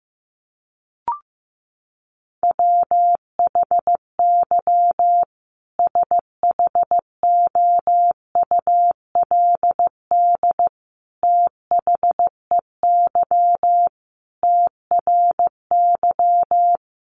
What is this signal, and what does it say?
Morse code, heard as a keyed tone, about 15 wpm, 700 Hz. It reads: WHY SHOULD THEY TRY
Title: Sentences from Top 1100 Words 15wpm